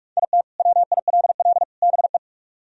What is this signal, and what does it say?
Morse code, heard as a keyed tone, about 41 wpm, 700 Hz. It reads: IT WILL BE